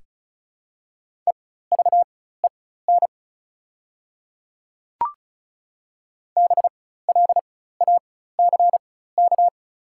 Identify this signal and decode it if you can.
Morse code, heard as a keyed tone, about 34 wpm, spaced as for 16 wpm, 700 Hz. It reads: EVEN BLACK
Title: 4x Character Spacing - Top 500 Words - 35wpm